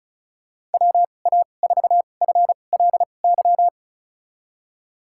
Morse code, heard as a keyed tone, about 36 wpm, spaced as for 25 wpm, 700 Hz. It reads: WA4FLY
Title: USA Call Signs - 6 Characters - 2x Character Spacing - 35wpm